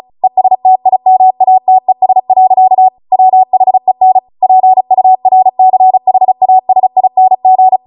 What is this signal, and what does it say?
Morse code, heard as a keyed tone, about 36 wpm, 750 Hz. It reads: ESTIMATES. WHEN PURCHASING
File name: Morse / 35WPM